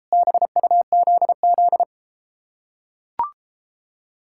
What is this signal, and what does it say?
Morse code, heard as a keyed tone, about 33 wpm, 700 Hz. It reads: BUZZ